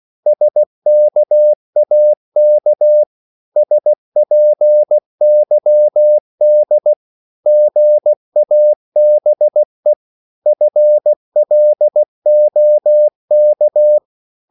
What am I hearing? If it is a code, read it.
Morse code, heard as a keyed tone, about 16 wpm, 600 Hz. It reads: SKAK SPYD GABE FLOK
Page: Korte danske ord 16wpm